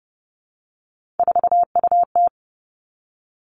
Morse code, heard as a keyed tone, about 30 wpm, 700 Hz. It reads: 4UT
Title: Three Challenging Letter-Number Combinations 30wpm